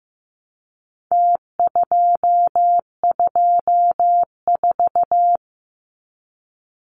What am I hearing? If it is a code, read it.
Morse code, heard as a keyed tone, about 15 wpm, 700 Hz. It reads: T224
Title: Numbers - Contest Style - 4-Digit 15wpm